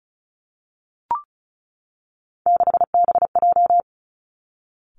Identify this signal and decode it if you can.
Morse code, heard as a keyed tone, about 36 wpm, 700 Hz. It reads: 6BJ
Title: Three Challenging Letter-Number Combinations 35wpm